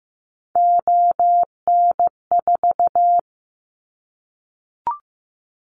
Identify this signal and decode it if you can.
Morse code, heard as a keyed tone, about 15 wpm, 700 Hz. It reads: ON4
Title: Callsign Prefixes for Top 50 DXCC - 15wpm